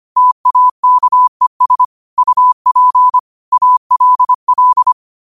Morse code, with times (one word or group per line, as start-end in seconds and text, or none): 0.2-1.8 TAKES
2.2-3.2 UP
3.5-4.9 ALL